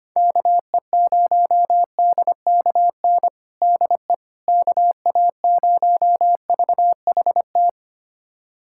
Morse code, heard as a keyed tone, about 25 wpm, 700 Hz. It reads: KE0DKN DE KA0VHT
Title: USA Callsign DE Callsign - 6 Characters - 25wpm